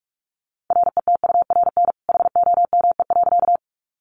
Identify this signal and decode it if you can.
Morse code, heard as a keyed tone, about 44 wpm, 700 Hz. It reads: RETURN HOME.
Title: Sentences from Top 800 Words 45wpm